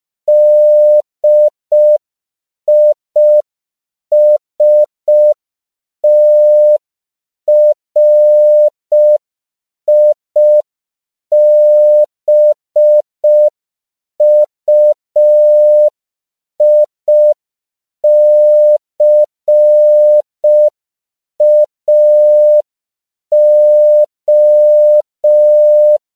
Morse code, heard as a keyed tone, about 5 wpm, 600 Hz. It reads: DISTRIBUICAO
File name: CW / 5PPM